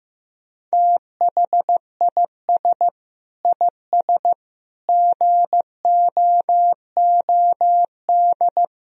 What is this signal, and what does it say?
Morse code, heard as a keyed tone, about 15 wpm, 700 Hz. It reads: THIS IS GOOD
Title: Sentences from Top 1000 Words - Review Part 4 - 15wpm